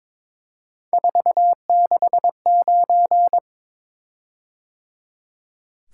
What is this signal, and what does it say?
Morse code, heard as a keyed tone, about 22 wpm, 700 Hz. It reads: 469
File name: practice / 22wpm / Three Letter-Number Combinations - 22wpm